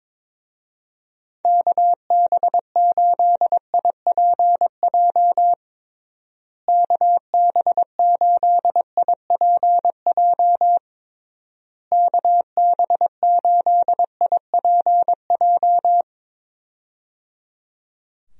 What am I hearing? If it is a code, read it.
Morse code, heard as a keyed tone, about 22 wpm, 700 Hz. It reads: KB8IPJ KB8IPJ KB8IPJ
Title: USA Call Signs - 6 Characters - Repeated 3x - 22wpm